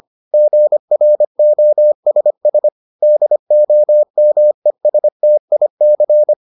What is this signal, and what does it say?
Morse code, heard as a keyed tone, about 25 wpm, 600 Hz. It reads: GROSS DOMESTIC